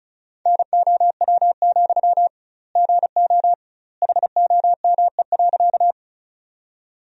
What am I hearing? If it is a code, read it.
Morse code, heard as a keyed tone, about 36 wpm, 700 Hz. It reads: NOW, GO HOME.
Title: 2x Word-Spacing - Sentences from Top 200 Words - 35wpm